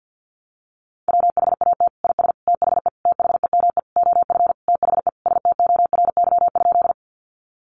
Morse code, heard as a keyed tone, about 50 wpm, 700 Hz. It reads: WHAT IS THE THEME OF THE STORY?
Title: Sentences from Top 1100 Words 50wpm